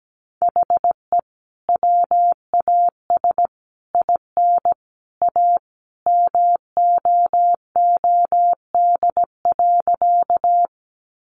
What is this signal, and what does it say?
Morse code, heard as a keyed tone, about 17 wpm, 700 Hz. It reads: HE WAS IN A MOOD.
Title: Sentences from Top 1100 Words 17wpm